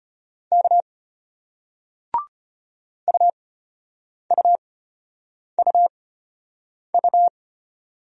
Morse code, heard as a keyed tone, about 34 wpm, spaced as for 8 wpm, 700 Hz. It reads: K UUUU